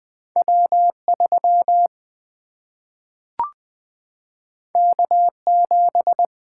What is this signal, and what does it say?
Morse code, heard as a keyed tone, about 20 wpm, 700 Hz. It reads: W3 K7
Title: Single-Letter Single-Number 20wpm